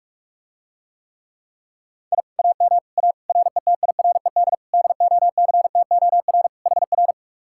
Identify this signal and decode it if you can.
Morse code, heard as a keyed tone, about 48 wpm, 700 Hz. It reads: I AM A RETIRED DOCTOR HR